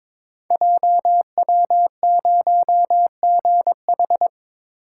Morse code, heard as a keyed tone, about 22 wpm, 700 Hz. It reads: JW0GH
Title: Code Groups - 5 Letter:Numbers - 22wpm